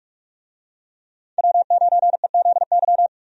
Morse code, heard as a keyed tone, about 45 wpm, 700 Hz. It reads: W9EZY
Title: USA Call Signs - 5 Characters - 45wpm